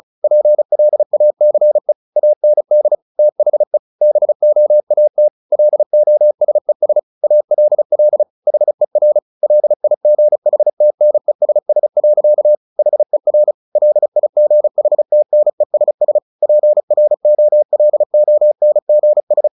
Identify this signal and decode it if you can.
Morse code, heard as a keyed tone, about 34 wpm, 600 Hz. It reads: PLACE AND THE BOAT LOSES ALL HER LIGHTNESS. HER LIGHTNESS PROLONGS
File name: the old man and the sea 2 35wpm